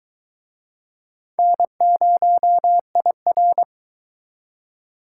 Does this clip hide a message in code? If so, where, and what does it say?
Yes: Morse code from 1.4 to 3.6 s, heard as a keyed tone, about 23 wpm, 700 Hz. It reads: N0IR